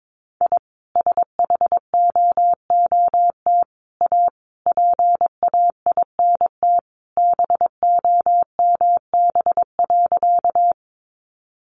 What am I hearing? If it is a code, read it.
Morse code, heard as a keyed tone, about 22 wpm, 700 Hz. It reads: I SHOOT A PAINT BOMB.
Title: A Sound Of Thunder 22wpm